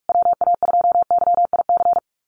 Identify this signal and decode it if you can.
Morse code, heard as a keyed tone, about 44 wpm, 700 Hz. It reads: WA2YIC
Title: USA Call Signs Repeated 3x 45wpm